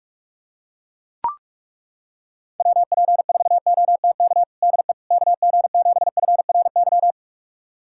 Morse code, heard as a keyed tone, about 42 wpm, 700 Hz. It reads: WP4OTK DE KG7FRY
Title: USA Callsign DE Callsign - 6 Characters - 45wpm